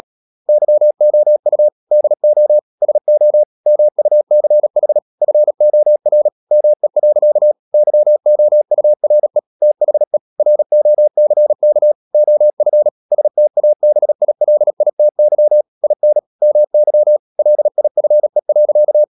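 Morse code, heard as a keyed tone, about 37 wpm, 600 Hz. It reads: YOU DO SO MUCH FOR ME. YOURE THE ROCK OF STABILITY IN MY LIFE.
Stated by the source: Morse practice at 36 wpm, text of Ken Follett's novel Never, from CW med Gnister.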